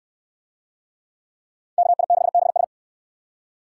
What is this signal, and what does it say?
Morse code, heard as a keyed tone, about 69 wpm, 700 Hz. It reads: KI6BS